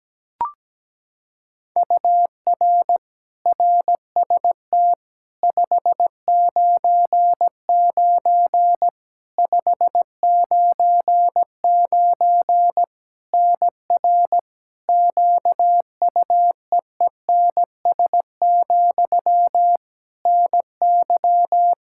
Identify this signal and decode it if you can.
Morse code, heard as a keyed tone, about 17 wpm, 700 Hz. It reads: UR RST 599 599 NR QUEENS, NY
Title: K1USN SST QSO Practice 17wpm